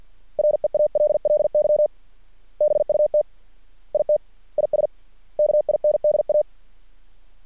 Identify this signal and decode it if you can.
Morse code, heard as a keyed tone, about 50 wpm, 600 Hz. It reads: REALLY BUT IT IS KINDA
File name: archived / CRSnet